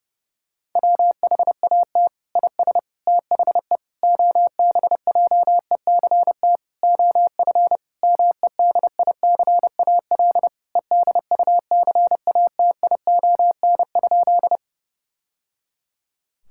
Morse code, heard as a keyed tone, about 32 wpm, 700 Hz. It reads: WHAT IS THE OBJECT OF MEDICAL EDUCATION?
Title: Sentences from Top 1000 Words - Review Part 6 - 30wpm